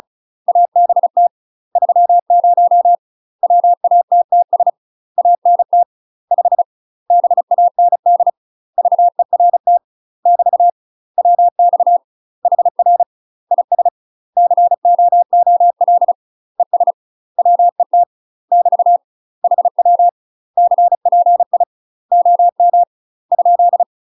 Morse code, heard as a keyed tone, about 34 wpm, 700 Hz. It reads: ABT 30 WATTS ANT 5 BAND VERT <BT> WX HR IS COOL ES WET <BT> HW CPI OM ?
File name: UK MASTER QSOs TRAINING / 35WPM / UK MASTER QSOs 0009